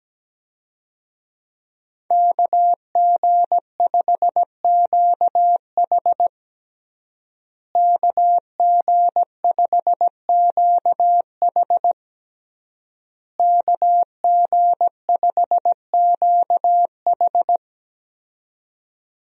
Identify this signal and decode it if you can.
Morse code, heard as a keyed tone, about 17 wpm, 700 Hz. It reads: KG5QH KG5QH KG5QH
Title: USA Call Signs - 5 Characters - Repeated 3x - 17wpm